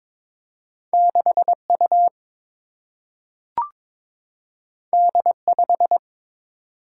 Morse code, heard as a keyed tone, about 22 wpm, 700 Hz. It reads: 6 U D 5
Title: Single-Letter Single-Number 22wpm